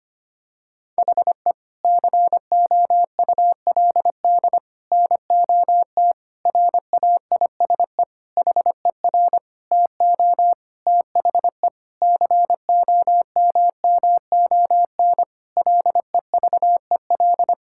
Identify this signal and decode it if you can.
Morse code, heard as a keyed tone, about 25 wpm, 700 Hz. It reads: HE COULD NOT RAISE HER TO THE COMMON LEVEL
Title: Sentences from Top 1600 Words 25wpm